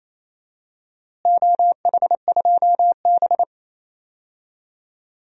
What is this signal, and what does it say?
Morse code, heard as a keyed tone, about 28 wpm, 700 Hz. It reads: OH2B
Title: DXCC-100 Call Signs - 4 Characters - Repeated 3x - 28wpm